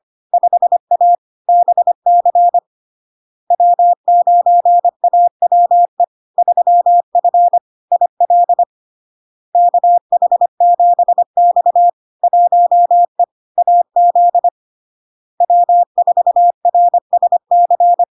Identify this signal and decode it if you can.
Morse code, heard as a keyed tone, about 25 wpm, 700 Hz. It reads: 5A BC W9AWE 3F IL KH7X 1E AZ W4RSC
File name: LARK-FD-2022-Log-in-CW-25wpm-700-Hz